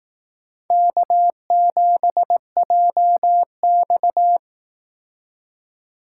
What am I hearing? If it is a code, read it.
Morse code, heard as a keyed tone, about 18 wpm, 700 Hz. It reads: K7JX
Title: Speed Racing - USA 1x2 Call Signs 15wpm